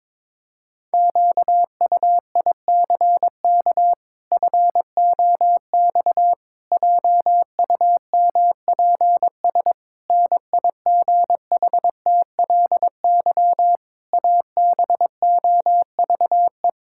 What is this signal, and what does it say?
Morse code, heard as a keyed tone, about 22 wpm, 700 Hz. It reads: QUICK FOX JUMPS NIGHTLY ABOVE
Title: Pangram Challenge v2 - 22wpm